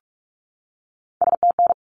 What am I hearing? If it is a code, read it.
Morse code, heard as a keyed tone, about 44 wpm, 700 Hz. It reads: STN
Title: Abbreviations 45wpm